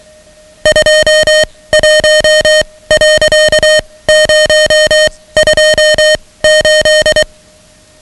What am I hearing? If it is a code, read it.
Morse code, heard as a keyed tone, about 23 wpm, spaced as for 17 wpm, 620 Hz, with noise. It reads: 21.028
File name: ham radio / CodeQuick / Code Quick 10